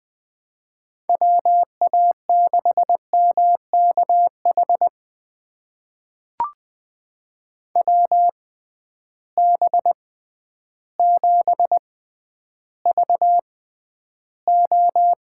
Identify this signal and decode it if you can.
Morse code, heard as a keyed tone, about 20 wpm, spaced as for 11 wpm, 700 Hz. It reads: WA6MKH WB7VO
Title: USA Call Signs - 6 Characters - 6x Character Spacing - 20wpm